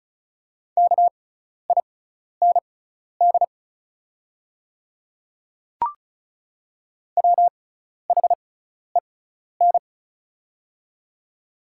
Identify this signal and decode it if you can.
Morse code, heard as a keyed tone, about 34 wpm, spaced as for 12 wpm, 690 Hz. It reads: KIND WHEN
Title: 6x Character Spacing - Top 500 Words - 35wpm